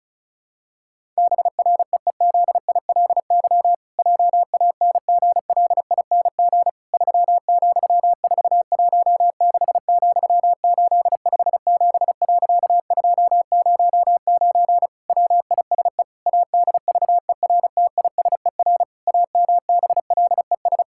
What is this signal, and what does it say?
Morse code, heard as a keyed tone, about 34 wpm, 700 Hz. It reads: BREEZILY JANGLING 3,416,857.209 WISE ADVERTISER AMBLES
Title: Pangram Challenge v2 - 35wpm